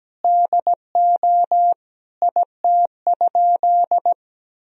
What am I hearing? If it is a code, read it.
Morse code, heard as a keyed tone, about 17 wpm, 700 Hz. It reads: DO IT?
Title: Sentences from Top 1000 Words - Review Part 4 - 17wpm